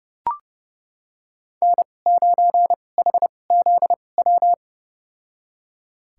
Morse code, heard as a keyed tone, about 32 wpm, spaced as for 22 wpm, 700 Hz. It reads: N9HZW
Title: USA Call Signs - 5 Characters - 2x Character Spacing - 30wpm